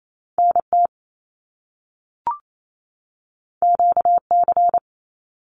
Morse code, heard as a keyed tone, about 28 wpm, 700 Hz. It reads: NT QC
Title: Canadian Provinces and Territories 28wpm